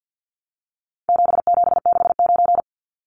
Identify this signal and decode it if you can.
Morse code, heard as a keyed tone, about 48 wpm, 700 Hz. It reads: 7769
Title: Numbers - 4-Digit 50wpm